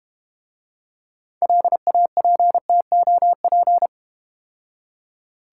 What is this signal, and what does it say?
Morse code, heard as a keyed tone, about 29 wpm, 690 Hz. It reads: LAPTOP
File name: Easy Words - Speed-Racing with Hints - 25wpm